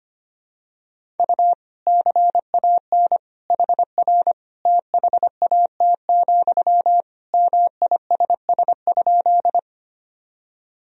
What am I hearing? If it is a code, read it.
Morse code, heard as a keyed tone, about 25 wpm, 700 Hz. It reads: U CAN HR THAT, MISS?
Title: Sentences from Top 500 Words and Abbreviations 25wpm